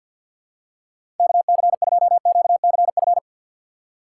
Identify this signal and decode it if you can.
Morse code, heard as a keyed tone, about 52 wpm, 690 Hz. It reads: KC2QCF
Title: USA Call Signs - 6 Characters - 3x Character Spacing - 50wpm